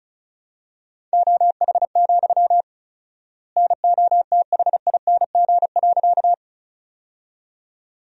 Morse code, heard as a keyed tone, about 36 wpm, 700 Hz. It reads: OH, NOTHING.
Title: 4x Word-Spacing - Sentences from Top 500 Words - 35wpm